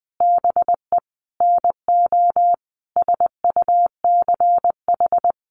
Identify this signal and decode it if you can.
Morse code, heard as a keyed tone, about 20 wpm, 700 Hz. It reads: BE NO SUCH
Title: Sentences from Top 600 Words 20wpm